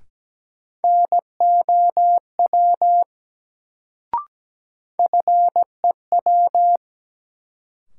Morse code, heard as a keyed tone, about 17 wpm, 700 Hz. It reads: NOW FEW
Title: Sets of 2 Words - New Word Drill - 17wpm